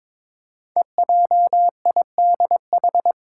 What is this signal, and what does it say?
Morse code, heard as a keyed tone, about 22 wpm, 700 Hz. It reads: EJIDH
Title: Code Groups - 5 Letters - 22wpm